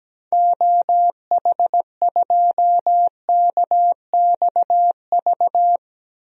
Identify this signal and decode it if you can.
Morse code, heard as a keyed tone, about 17 wpm, 700 Hz. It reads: OH2KXV
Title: DXCC-100 Call Signs - 6 Characters - 17wpm